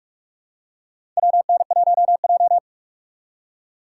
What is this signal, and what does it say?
Morse code, heard as a keyed tone, about 46 wpm, 700 Hz. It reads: WN1J
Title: USA Call Signs - 6x Character Spacing - 45wpm